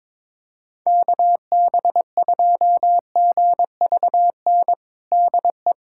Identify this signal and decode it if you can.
Morse code, heard as a keyed tone, about 22 wpm, 700 Hz. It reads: KB2GVN DE
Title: USA Callsign DE Callsign - 6 Characters - 22wpm